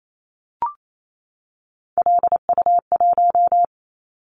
Morse code, heard as a keyed tone, about 28 wpm, 700 Hz. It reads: LU1
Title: Callsign Prefixes for Top 100 DXCC - 28wpm